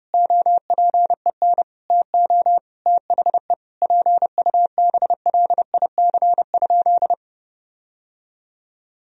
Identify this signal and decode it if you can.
Morse code, heard as a keyed tone, about 30 wpm, 700 Hz. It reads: OPEN TO THE PUBLIC?
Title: Sentences from Top 1000 Words - Review Part 3 - 30wpm